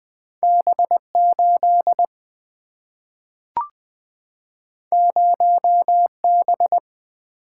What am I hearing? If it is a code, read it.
Morse code, heard as a keyed tone, about 20 wpm, 700 Hz. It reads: B8 0B
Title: Single-Letter Single-Number 20wpm